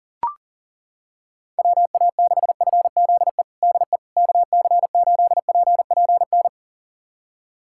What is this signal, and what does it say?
Morse code, heard as a keyed tone, about 40 wpm, 700 Hz. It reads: WA6FZE DE KC8PPN
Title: USA Callsign DE Callsign - 6 Characters - 40wpm